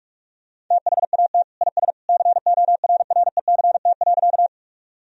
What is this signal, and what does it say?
Morse code, heard as a keyed tone, about 46 wpm, 700 Hz. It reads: THAT IS CORRECT.
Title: Sentences from Top 1000 Words - Review Part 2 - 45wpm